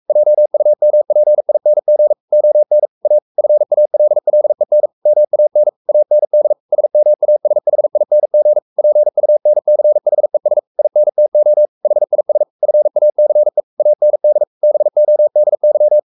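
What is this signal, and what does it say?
Morse code, heard as a keyed tone, about 44 wpm, 600 Hz. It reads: JUMPING ON A FALLEN MAN AND SMASHING PUNCHES INTO HIS FACE AND BODY
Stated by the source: Morse code at 43 wpm sending Ken Follett's novel Never in English, from CW med Gnister.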